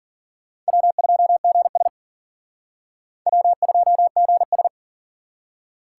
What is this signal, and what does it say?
Morse code, heard as a keyed tone, about 47 wpm, 700 Hz. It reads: W2GS W2GS
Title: Speed Racing - USA Call Signs 40wpm